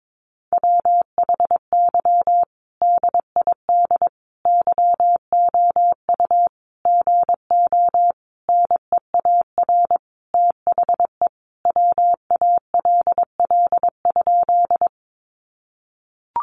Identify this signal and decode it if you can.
Morse code, heard as a keyed tone, about 22 wpm, 700 Hz. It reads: WHY DID YOU GO NEAR THE WALL?
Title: Sentences from Top 1100 Words 22wpm